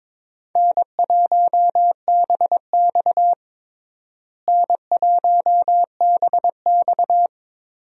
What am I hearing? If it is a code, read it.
Morse code, heard as a keyed tone, about 22 wpm, 700 Hz. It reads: N1BX N1BX
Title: USA Call Signs - 4 Characters - Repeated 3x - 22wpm